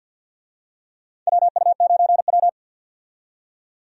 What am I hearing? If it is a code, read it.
Morse code, heard as a keyed tone, about 51 wpm, 700 Hz. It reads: WU9W
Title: USA Call Signs - 4 Characters - 50wpm